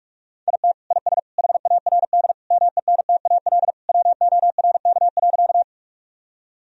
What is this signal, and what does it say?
Morse code, heard as a keyed tone, about 42 wpm, 700 Hz. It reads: IT IS HARD MENTAL WORK.
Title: Sentences from Top 1000 Words - Review Part 4 - 45wpm